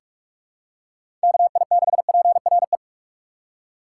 Morse code, heard as a keyed tone, about 46 wpm, 700 Hz. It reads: KI6PRE
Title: USA Call Signs - 4x Character Spacing - 45wpm